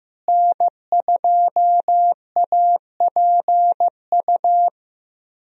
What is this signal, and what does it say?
Morse code, heard as a keyed tone, about 15 wpm, 700 Hz. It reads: N2APU
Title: USA Call Signs - 5 Characters - Repeated 2x - 15wpm